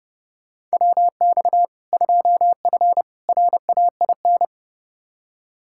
Morse code, heard as a keyed tone, about 30 wpm, 700 Hz. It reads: WX 2F RAIN
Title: K1USN SST QSO Practice 30wpm